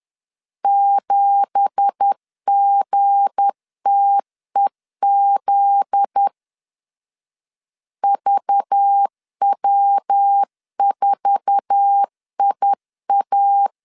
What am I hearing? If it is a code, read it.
Morse code, heard as a keyed tone, about 10 wpm, 800 Hz. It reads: 7GTEZ VW4IA
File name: LEZIONI / 9 2867 10wpm 2733